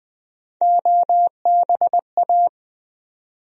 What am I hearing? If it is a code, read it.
Morse code, heard as a keyed tone, about 20 wpm, 700 Hz. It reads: OBA
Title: USA Call Sign Postfixes - 3 Letters 20wpm